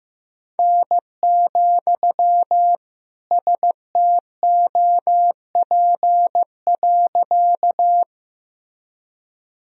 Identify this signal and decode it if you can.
Morse code, heard as a keyed tone, about 15 wpm, 700 Hz. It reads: N, STOP.
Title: Sentences from Top 300 Words and Abbreviations 15wpm